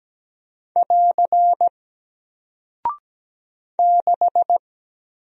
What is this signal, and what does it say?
Morse code, heard as a keyed tone, about 17 wpm, 700 Hz. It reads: <AR> 6
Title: Single Character (Letter:Number:Punctuation:Prosign) Alt 17wpm